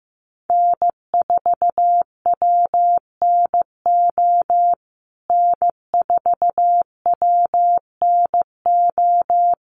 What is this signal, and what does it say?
Morse code, heard as a keyed tone, about 14 wpm, 700 Hz. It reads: N4WNO N4WNO
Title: USA Call Signs Repeated 2x 15wpm